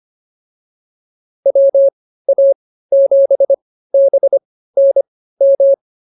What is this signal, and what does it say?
Morse code, heard as a keyed tone, about 24 wpm, spaced as for 15 wpm, 550 Hz. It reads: WA7BNM